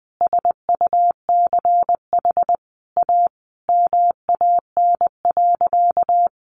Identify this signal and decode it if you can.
Morse code, heard as a keyed tone, about 20 wpm, 700 Hz. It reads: SUCH A MAN.
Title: Sentences from Top 1300 Words 20wpm